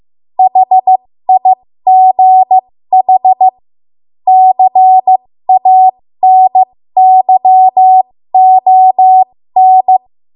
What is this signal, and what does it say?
Morse code, heard as a keyed tone, about 15 wpm, spaced as for 13 wpm, 750 Hz. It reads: HIGH CANYON
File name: Archive / 13 WPM / 13WPM